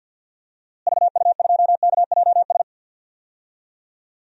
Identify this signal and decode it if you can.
Morse code, heard as a keyed tone, about 51 wpm, 700 Hz. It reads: VU2KJS
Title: DXCC-100 Call Signs - 6 Characters - Repeated 3x - 50wpm